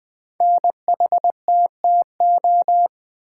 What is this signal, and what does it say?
Morse code, heard as a keyed tone, about 20 wpm, 700 Hz. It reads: NHTTO
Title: Code Groups - 5 Letters - 20wpm